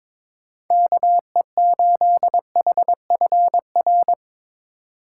Morse code, heard as a keyed tone, about 22 wpm, 700 Hz. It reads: KE8HFR
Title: USA Call Signs - 6 Characters - 2x Character Spacing - 22wpm